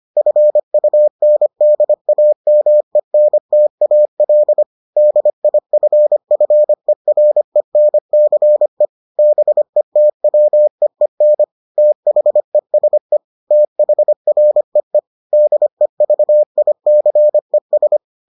Morse code, heard as a keyed tone, about 25 wpm, 600 Hz. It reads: FUNDAMENTAL DIFFERENCE BETWEEN THESE THREE DEVICES